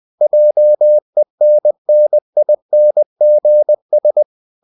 Morse code, heard as a keyed tone, about 20 wpm, 600 Hz. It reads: JENNINGS
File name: CW / Practice / EnglishQuotations0320